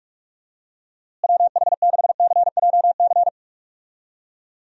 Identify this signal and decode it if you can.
Morse code, heard as a keyed tone, about 41 wpm, 700 Hz. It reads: WH6CJC
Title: USA Call Signs - 6 Characters - Repeated 3x - 45wpm